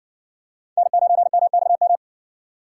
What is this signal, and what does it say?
Morse code, heard as a keyed tone, about 56 wpm, 700 Hz. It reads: N9MXM